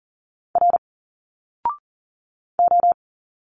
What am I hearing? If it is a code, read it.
Morse code, heard as a keyed tone, about 40 wpm, 700 Hz. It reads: R O
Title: Single Letters 40wpm